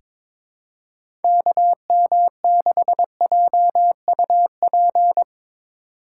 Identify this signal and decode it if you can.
Morse code, heard as a keyed tone, about 22 wpm, 700 Hz. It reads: KM6JUP
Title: USA Call Signs - 6 Characters - 8x Character Spacing - 22wpm